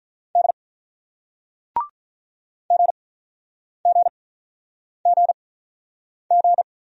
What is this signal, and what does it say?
Morse code, heard as a keyed tone, about 48 wpm, spaced as for 9 wpm, 700 Hz. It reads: D GGGG